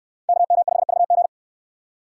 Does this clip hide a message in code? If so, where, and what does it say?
Yes: Morse code from 0.3 to 1.3 s, heard as a keyed tone, about 68 wpm, 700 Hz. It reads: KM5VG